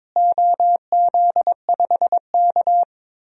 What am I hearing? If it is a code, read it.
Morse code, heard as a keyed tone, about 22 wpm, 700 Hz. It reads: OZ5K
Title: DXCC-100 Call Signs - 4 Characters - Repeated 3x - 22wpm